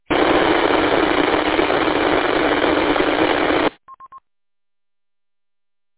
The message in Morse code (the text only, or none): none